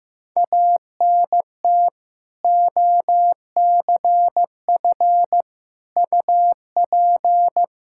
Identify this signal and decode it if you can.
Morse code, heard as a keyed tone, about 15 wpm, 700 Hz. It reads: ANT OCF UP
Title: Practice QSO - Exchange 2 - Info - 15wpm